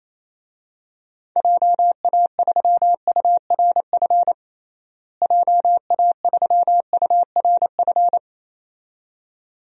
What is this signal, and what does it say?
Morse code, heard as a keyed tone, about 28 wpm, 700 Hz. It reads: JA3URF JA3URF
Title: DXCC-100 Call Signs - 6 Characters - Repeated 2x - 28wpm